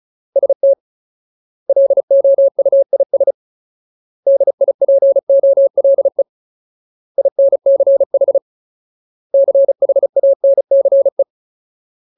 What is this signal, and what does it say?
Morse code, heard as a keyed tone, about 34 wpm, 550 Hz. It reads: ST LOUIS DIPOLE INCH CHANCE
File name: Ultimate Word List / 35WPM Ultimate